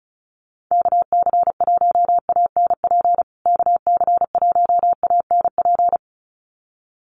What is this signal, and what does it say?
Morse code, heard as a keyed tone, about 34 wpm, 700 Hz. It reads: KC1ANP KC1ANP
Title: USA Call Signs Repeated 2x 35wpm